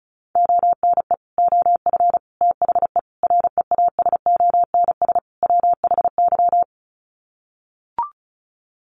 Morse code, heard as a keyed tone, about 34 wpm, 700 Hz. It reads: ONE OF THE REASONS WHY
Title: Sets of 5 Words 35wpm